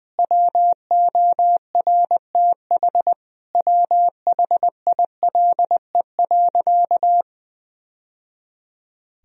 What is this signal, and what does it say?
Morse code, heard as a keyed tone, about 21 wpm, 700 Hz. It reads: WORTH WHILE.
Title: Sentences from Top 1000 Words - Review Part 1 - 20wpm